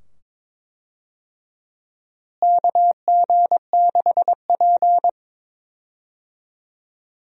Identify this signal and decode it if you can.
Morse code, heard as a keyed tone, about 22 wpm, 700 Hz. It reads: KG6P
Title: USA Call Signs - 4 Characters - Repeated 2x - 22wpm